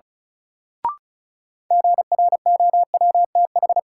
Morse code, heard as a keyed tone, about 34 wpm, 700 Hz. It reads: GROWTH
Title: Sentences from Top 1000 Words - Review Part 5 - New Word Drill - 35wpm